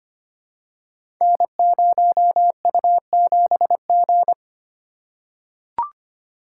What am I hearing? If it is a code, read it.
Morse code, heard as a keyed tone, about 25 wpm, 700 Hz. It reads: N0U7G
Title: Code Groups - 5 Letter:Numbers - 25wpm